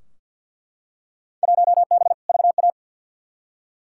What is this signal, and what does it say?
Morse code, heard as a keyed tone, about 50 wpm, 700 Hz. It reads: 1B VA
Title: Field Day Exchanges - 50wpm